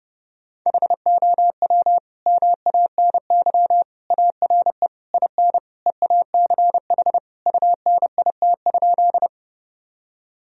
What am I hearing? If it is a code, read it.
Morse code, heard as a keyed tone, about 32 wpm, 700 Hz. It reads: HOW MANY ARE IN EACH UNIT?
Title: Sentences from Top 1000 Words - Review Part 2 - 30wpm